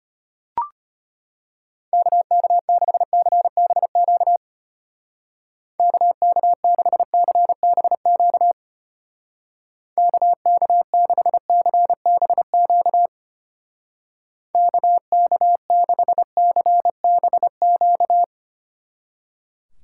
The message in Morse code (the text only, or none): KK6CBQ KK6CBQ KK6CBQ KK6CBQ